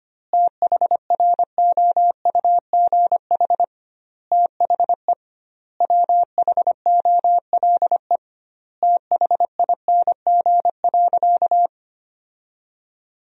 Morse code, heard as a keyed tone, about 25 wpm, 700 Hz. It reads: THROUGH THE WHOLE THING.
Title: 2x Word-Spacing - Sentences from Top 400 Words - 25wpm